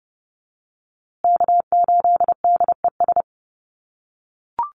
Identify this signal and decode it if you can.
Morse code, heard as a keyed tone, about 30 wpm, 700 Hz. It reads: K8DES
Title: USA Call Signs Repeated 3x 30wpm